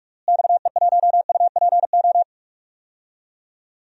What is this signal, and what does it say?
Morse code, heard as a keyed tone, about 46 wpm, 700 Hz. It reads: XE1UPO
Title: DXCC-100 Call Signs - 6 Characters - Repeated 3x - 45wpm